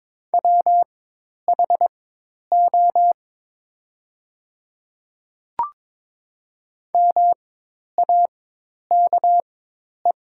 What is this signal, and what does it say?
Morse code, heard as a keyed tone, about 22 wpm, spaced as for 10 wpm, 700 Hz. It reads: WHO MAKE
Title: 4x Character Spacing - Top 100 Words - 22wpm